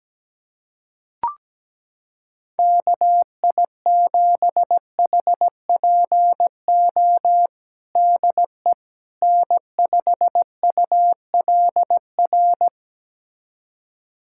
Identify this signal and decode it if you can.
Morse code, heard as a keyed tone, about 17 wpm, 700 Hz. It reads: KI7HPO DE N5ULR
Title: USA Callsign DE Callsign 17wpm